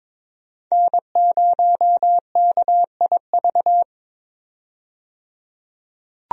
Morse code, heard as a keyed tone, about 21 wpm, 700 Hz. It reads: N0KIV
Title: USA Call Signs - 5 Characters - 22wpm